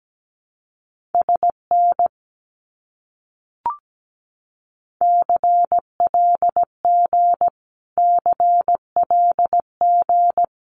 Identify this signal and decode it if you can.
Morse code, heard as a keyed tone, about 17 wpm, 700 Hz. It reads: SN CLG CLG
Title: Abbreviations Repeated 3x - 17wpm